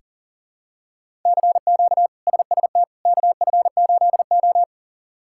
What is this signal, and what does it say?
Morse code, heard as a keyed tone, about 40 wpm, 700 Hz. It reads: CQ SST KF8O
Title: K1USN SST QSO Practice - 40wpm